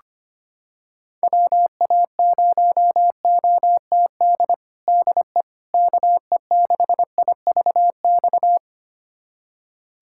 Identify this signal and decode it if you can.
Morse code, heard as a keyed tone, about 25 wpm, 700 Hz. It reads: WA0OTD DE KE6IVX
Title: USA Callsign DE Callsign 25wpm